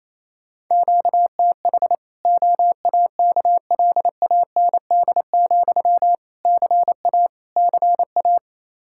Morse code, heard as a keyed tone, about 28 wpm, 700 Hz. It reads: QTH OAKLAND, CA CA
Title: Practice QSO - Exchange 1 - Beginning - 28wpm